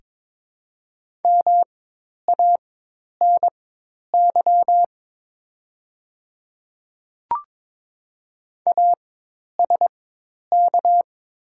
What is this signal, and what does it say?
Morse code, heard as a keyed tone, about 22 wpm, spaced as for 10 wpm, 700 Hz. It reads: MANY ASK
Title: 4x Character Spacing - Top 200 Words - 22wpm